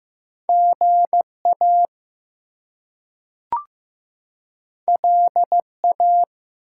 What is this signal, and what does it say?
Morse code, heard as a keyed tone, about 15 wpm, 700 Hz. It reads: GA LA
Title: US States and Canadian Provinces 15wpm